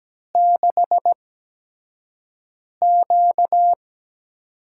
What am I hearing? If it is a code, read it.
Morse code, heard as a keyed tone, about 17 wpm, spaced as for 5 wpm, 700 Hz. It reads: 6Q
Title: USA Call Signs - 4 Characters - 8x Character Spacing - 17wpm